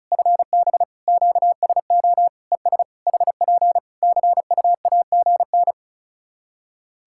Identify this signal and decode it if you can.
Morse code, heard as a keyed tone, about 36 wpm, 700 Hz. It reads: FB QSO ES HP CUAGN
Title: K1USN SST QSO Practice 35wpm